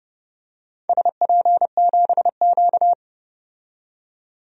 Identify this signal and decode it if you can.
Morse code, heard as a keyed tone, about 30 wpm, 700 Hz. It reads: SP7Q
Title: DXCC-100 Call Signs - 4 Characters - 30wpm